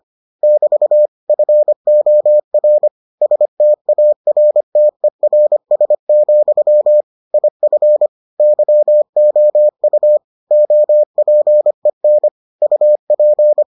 Morse code, heard as a keyed tone, about 25 wpm, 600 Hz. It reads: <BT> FOR STARTERS, IF YOU OPEN UP